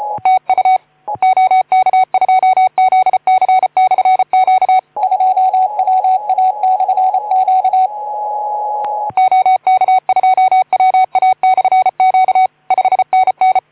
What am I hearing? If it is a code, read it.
Morse code, heard as a keyed tone, about 32 wpm, 750 Hz. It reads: TU OK2ZC/Q 2WA/Q OK2WA/Q 5NN
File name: Archiv / CUC181 / OK2ZC CUC 181